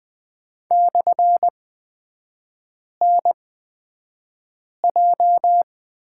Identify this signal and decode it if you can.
Morse code, heard as a keyed tone, about 20 wpm, spaced as for 5 wpm, 700 Hz. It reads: /NJ
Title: 224 Lesson 36e - BT - Three Characters 20wpm